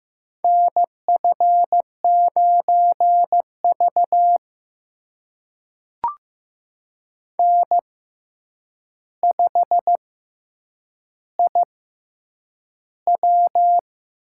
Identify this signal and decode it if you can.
Morse code, heard as a keyed tone, about 15 wpm, spaced as for 8 wpm, 700 Hz. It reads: NF9V N5IW
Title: USA Call Signs - 4 Characters - 6x Character Spacing - 15wpm